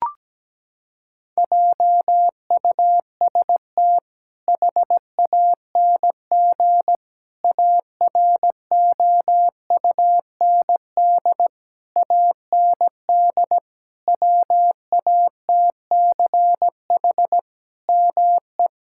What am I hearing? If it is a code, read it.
Morse code, heard as a keyed tone, about 17 wpm, 700 Hz. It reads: JUST HANG AROUND AND WATCH ME
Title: Sentences from Top 1600 Words 17wpm